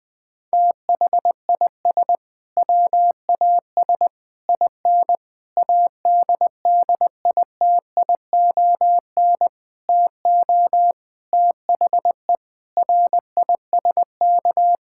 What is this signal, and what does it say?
Morse code, heard as a keyed tone, about 21 wpm, 700 Hz. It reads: THIS WAS IN ADDITION TO THE RISK
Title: Sentences from Top 1000 Words - Review Part 6 - 20wpm